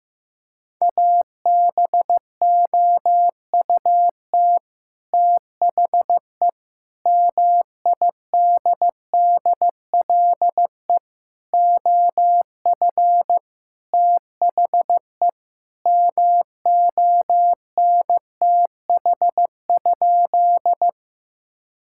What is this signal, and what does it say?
Morse code, heard as a keyed tone, about 15 wpm, 700 Hz. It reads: ABOUT THE MIDDLE OF THE MONTH?
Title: Sentences from Top 1000 Words - Review Part 6 - 15wpm